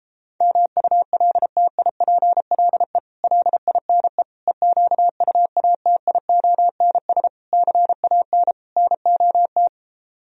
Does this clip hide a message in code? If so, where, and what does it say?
Yes: Morse code from 0.4 to 9.7 s, heard as a keyed tone, about 33 wpm, 700 Hz. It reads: MULTIPLE LINE EQUATIONS CAN NOT